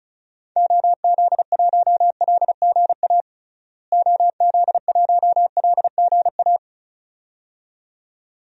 Morse code, heard as a keyed tone, about 34 wpm, 700 Hz. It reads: OZ1LGA OZ1LGA
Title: DXCC-100 Call Signs - 6 Characters - Repeated 2x - 35wpm